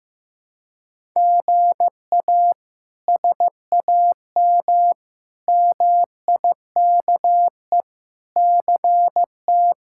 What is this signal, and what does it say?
Morse code, heard as a keyed tone, about 15 wpm, 700 Hz. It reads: GA SAM MIKE CT
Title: K1USN SST QSO Practice - 15wpm